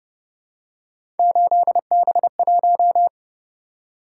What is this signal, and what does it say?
Morse code, heard as a keyed tone, about 28 wpm, 700 Hz. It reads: 8B1